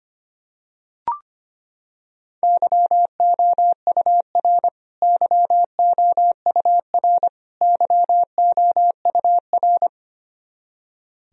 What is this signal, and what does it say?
Morse code, heard as a keyed tone, about 25 wpm, 700 Hz. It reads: YOUR YOUR YOUR
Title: Top 100 Words Repeated 25wpm